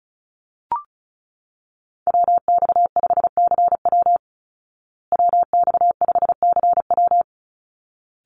Morse code, heard as a keyed tone, about 34 wpm, 700 Hz. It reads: WX5CW WX5CW
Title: Call Signs Repeated 3x - 35wpm